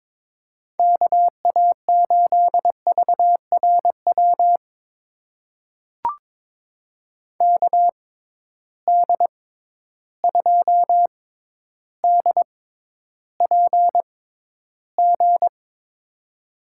Morse code, heard as a keyed tone, about 22 wpm, spaced as for 11 wpm, 700 Hz. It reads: KA8VRW KD2DPG
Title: USA Call Signs - 6 Characters - 6x Character Spacing - 22wpm